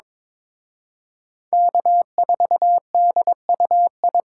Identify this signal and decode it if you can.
Morse code, heard as a keyed tone, about 26 wpm, 700 Hz. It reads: K4DUI
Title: Speed Racing - USA Call Signs 22wpm